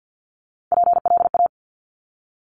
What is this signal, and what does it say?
Morse code, heard as a keyed tone, about 50 wpm, 700 Hz. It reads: PLA
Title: Top Three Letter Combinations in Words 50wpm